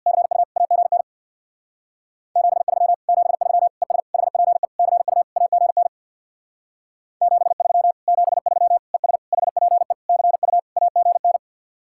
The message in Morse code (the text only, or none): CU AGN 73 73 ES HPE CU AGN 73 73 ES HPE CU AGN